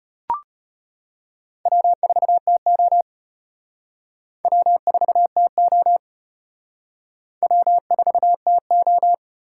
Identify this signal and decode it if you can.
Morse code, heard as a keyed tone, about 33 wpm, 700 Hz. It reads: W4TO W4TO W4TO